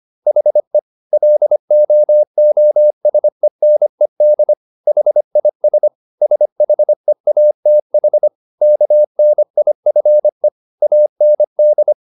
Morse code, heard as a keyed tone, about 25 wpm, 600 Hz. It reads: HE LOOSENED HIS SHEATH KNIFE AND
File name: the old man and the sea part 1 25wpm